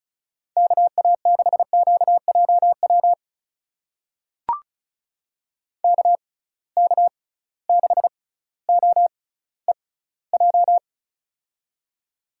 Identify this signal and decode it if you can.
Morse code, heard as a keyed tone, about 36 wpm, spaced as for 18 wpm, 700 Hz. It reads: KA6QJW KK6OEJ
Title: USA Call Signs - 6 Characters - 6x Character Spacing - 35wpm